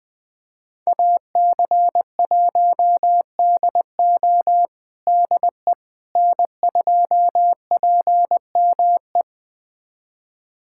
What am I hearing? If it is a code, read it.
Morse code, heard as a keyed tone, about 20 wpm, 700 Hz. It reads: AC1DO DE N2PME
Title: USA Callsign DE Callsign - 5 Characters - 20wpm